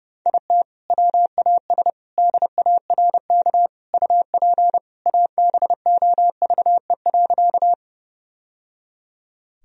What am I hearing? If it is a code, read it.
Morse code, heard as a keyed tone, about 30 wpm, 700 Hz. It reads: IT WAS DARK UP ABOVE.
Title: Sentences from Top 1000 Words - Review Part 2 - 30wpm